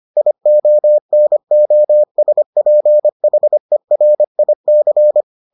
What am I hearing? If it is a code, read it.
Morse code, heard as a keyed tone, about 25 wpm, 600 Hz. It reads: IONOSPHERIC